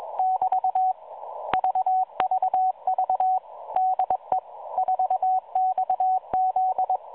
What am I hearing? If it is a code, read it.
Morse code, heard as a keyed tone, about 21 wpm, 750 Hz, with radio noise. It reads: <BT> VVV DE 4XZ